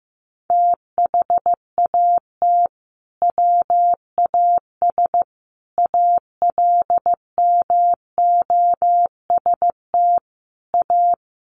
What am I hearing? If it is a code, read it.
Morse code, heard as a keyed tone, about 15 wpm, 700 Hz. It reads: THAT WAS ALMOST A
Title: Sentences from Top 1400 Words 15wpm